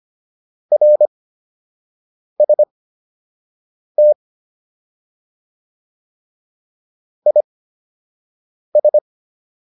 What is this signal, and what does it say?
Morse code, heard as a keyed tone, about 25 wpm, spaced as for 6 wpm, 600 Hz. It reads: RST IS
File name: QSO 203 06